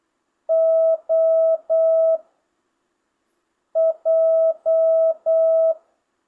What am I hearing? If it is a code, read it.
Morse code, heard as a keyed tone, about 8 wpm, 650 Hz. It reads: O J
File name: jh8cbh / 2025cw